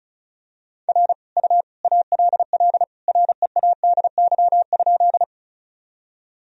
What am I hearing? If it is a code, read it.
Morse code, heard as a keyed tone, about 34 wpm, 700 Hz. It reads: R U ALL READY?
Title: Sentences from Top 400 Words and Abbreviations 35wpm